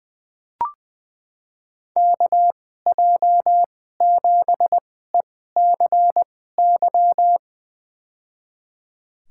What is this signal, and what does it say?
Morse code, heard as a keyed tone, about 21 wpm, spaced as for 15 wpm, 700 Hz. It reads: KJ7ECY
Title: USA Call Signs - 2x Character Spacing - 20wpm